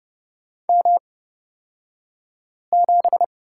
Morse code, heard as a keyed tone, about 30 wpm, spaced as for 5 wpm, 700 Hz. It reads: M7
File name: Lesson 36d - BT - Two Characters 30wpm